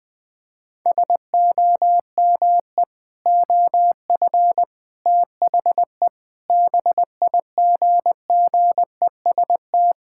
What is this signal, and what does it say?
Morse code, heard as a keyed tone, about 20 wpm, 700 Hz. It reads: SOME OF THE BIGGEST
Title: Sentences with 7-Word Phrases - Spoken Beginning - 20wpm